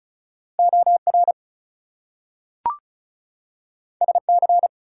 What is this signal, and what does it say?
Morse code, heard as a keyed tone, about 36 wpm, 700 Hz. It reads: OR SC
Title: US States and Canadian Provinces 35wpm